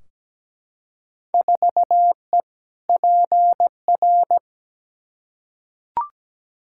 Morse code, heard as a keyed tone, about 17 wpm, 700 Hz. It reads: 4E PR
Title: Field Day Exchanges Encore - 17wpm